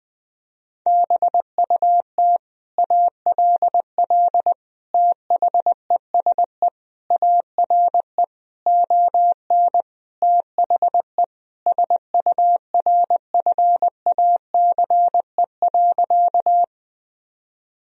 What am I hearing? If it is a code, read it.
Morse code, heard as a keyed tone, about 20 wpm, 700 Hz. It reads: BUT ALL THESE ARE ON THE SURFACE.
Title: Sentences from Top 1000 Words - Review Part 3 - 20wpm